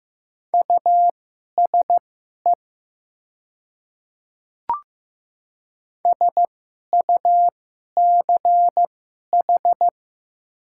Morse code, heard as a keyed tone, about 15 wpm, spaced as for 11 wpm, 700 Hz. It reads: USE SUCH
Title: 2x Character Spacing - Top 300 Words - 15wpm